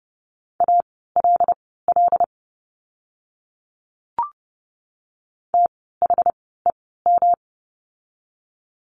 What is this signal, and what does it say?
Morse code, heard as a keyed tone, about 30 wpm, spaced as for 17 wpm, 700 Hz. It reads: ALL THEM
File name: course / 30:10wpm / 237 Lesson 38f - Letter q - Words 30wpm